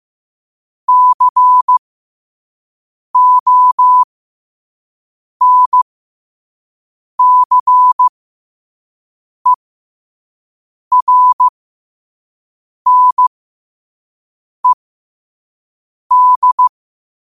Morse code, heard as a keyed tone, about 15 wpm, spaced as for 5 wpm, 1000 Hz. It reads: CONCERNED
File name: qotdfiles / 5wpm